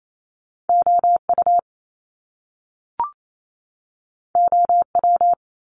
Morse code, heard as a keyed tone, about 28 wpm, 700 Hz. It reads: O U O W
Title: Top Two Letter Combinations in Words 28wpm